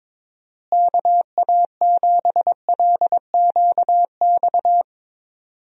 Morse code, heard as a keyed tone, about 21 wpm, 700 Hz. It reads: KA7LQX
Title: USA Call Signs - 6 Characters - Repeated 2x - 22wpm